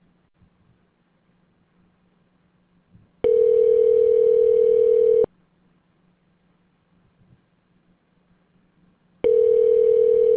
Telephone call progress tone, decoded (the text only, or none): ringback tone